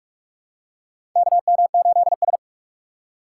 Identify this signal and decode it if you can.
Morse code, heard as a keyed tone, about 42 wpm, 700 Hz. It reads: KM8S
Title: USA Call Signs - 4 Characters - 4x Character Spacing - 45wpm